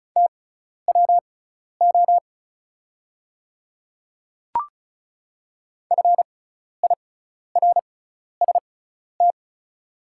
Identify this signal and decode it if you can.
Morse code, heard as a keyed tone, about 36 wpm, spaced as for 12 wpm, 700 Hz. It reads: TWO FIRST
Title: 6x Character Spacing - Top 100 Words - 35wpm